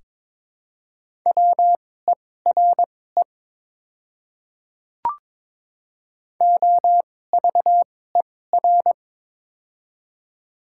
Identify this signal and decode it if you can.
Morse code, heard as a keyed tone, about 22 wpm, spaced as for 16 wpm, 700 Hz. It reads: WERE OVER
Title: 2x Character Spacing - Top 100 Words - 22wpm